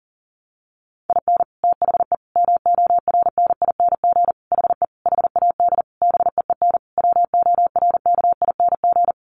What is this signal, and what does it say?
Morse code, heard as a keyed tone, about 37 wpm, 700 Hz. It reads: IN THE MORNING HE HAD BEEN WORKING
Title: Sentences from Top 1200 Words 40wpm